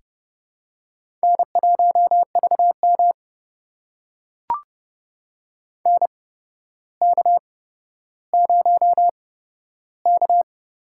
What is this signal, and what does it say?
Morse code, heard as a keyed tone, about 30 wpm, spaced as for 13 wpm, 700 Hz. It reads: N1VM NK0K
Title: USA Call Signs - 4 Characters - 8x Character Spacing - 30wpm